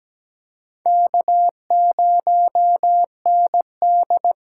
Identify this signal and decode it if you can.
Morse code, heard as a keyed tone, about 17 wpm, 700 Hz. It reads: K0ND
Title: USA Call Signs - 4x Character Spacing - 17wpm